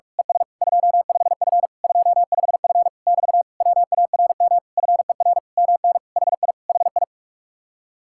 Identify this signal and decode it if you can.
Morse code, heard as a keyed tone, about 46 wpm, 700 Hz. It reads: ES 25F 25F <BT> WARM FER MN HI HI
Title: Speed-Racing Common QSO Phrases Encore - 45wpm